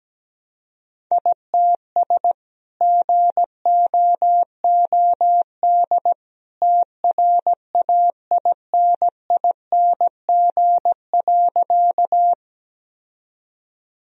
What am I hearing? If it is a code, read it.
Morse code, heard as a keyed tone, about 17 wpm, 700 Hz. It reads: ITS GOOD TRAINING.
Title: Sentences from Top 1000 Words - Review Part 4 - 17wpm